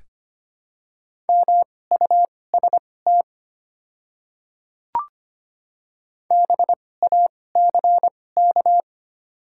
Morse code, heard as a keyed tone, about 25 wpm, spaced as for 18 wpm, 700 Hz. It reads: MUST BACK
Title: 2x Character Spacing - Top 200 Words - 25wpm